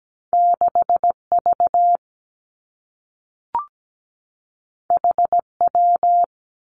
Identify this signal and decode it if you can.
Morse code, heard as a keyed tone, about 17 wpm, 700 Hz. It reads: 6 V H W
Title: Two Challenging Letter-Number Combinations 17wpm